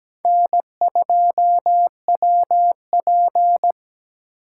Morse code, heard as a keyed tone, about 17 wpm, 700 Hz. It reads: N2WP
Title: USA Call Signs - 2x Character Spacing - 17wpm